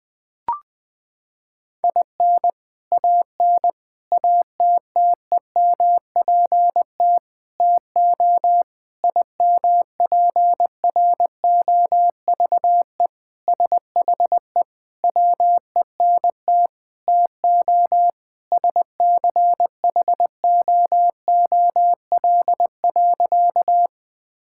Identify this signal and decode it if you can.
Morse code, heard as a keyed tone, about 20 wpm, 700 Hz. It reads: IN AN ATTEMPT TO IMPROVE SHE WENT TO SCHOOL.
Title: Sentences with 4-Word Phrases - 20wpm